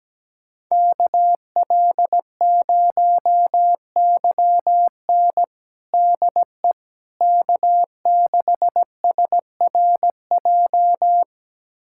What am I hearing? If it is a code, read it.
Morse code, heard as a keyed tone, about 17 wpm, 700 Hz. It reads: KL0YN DE K6SRJ
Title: USA Callsign DE Callsign - 5 Characters - 17wpm